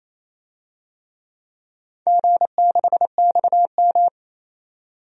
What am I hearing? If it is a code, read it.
Morse code, heard as a keyed tone, about 28 wpm, 700 Hz. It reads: G6XM
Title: DXCC-100 Call Signs - 4 Characters - 28wpm